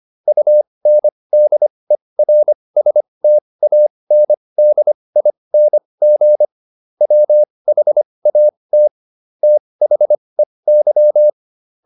Morse code, heard as a keyed tone, about 25 wpm, spaced as for 20 wpm, 600 Hz. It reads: UNDERSTANDING WHAT THEY